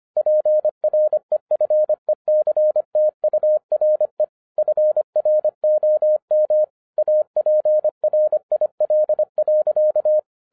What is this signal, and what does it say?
Morse code, heard as a keyed tone, about 25 wpm, 610 Hz. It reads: PREFECTURE FROM APRIL.
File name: 2011-10-mp3 / n04-JUST MOVED 25wpm